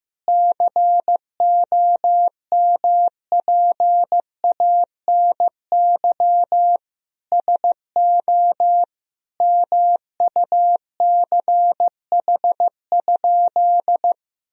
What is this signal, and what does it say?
Morse code, heard as a keyed tone, about 15 wpm, 700 Hz. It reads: COMPANY SO MUCH?
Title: Sentences from Top 1000 Words - Review Part 3 - 15wpm